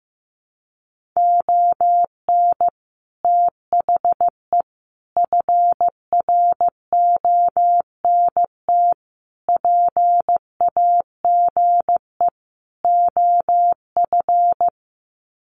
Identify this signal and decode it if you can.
Morse code, heard as a keyed tone, about 15 wpm, 700 Hz. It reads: ON THE FRONT PAGE OF
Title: Sets of 5 Words Encore 15wpm